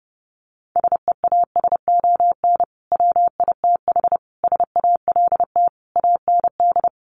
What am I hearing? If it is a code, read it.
Morse code, heard as a keyed tone, about 30 wpm, 700 Hz. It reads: SEASON WITH SALT AND
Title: Sets of 5 Words 30wpm